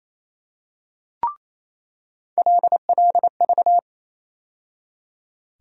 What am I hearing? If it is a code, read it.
Morse code, heard as a keyed tone, about 28 wpm, 700 Hz. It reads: LLV
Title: USA Call Sign Postfixes - 3 Letters 28wpm